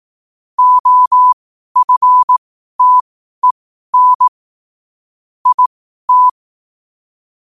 Morse code, heard as a keyed tone, about 18 wpm, spaced as for 12 wpm, 1000 Hz. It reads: OFTEN IT
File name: 12wpm